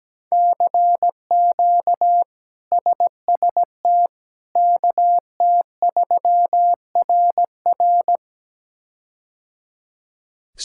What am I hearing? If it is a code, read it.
Morse code, heard as a keyed tone, about 17 wpm, 700 Hz. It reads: CQ SST KT3RR
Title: K1USN SST QSO Practice - 17wpm